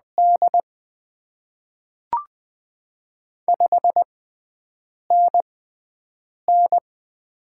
Morse code, heard as a keyed tone, about 20 wpm, spaced as for 7 wpm, 700 Hz. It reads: D 5NN
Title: Top 20 QSO Elements - 6x Character Spacing - 20wpm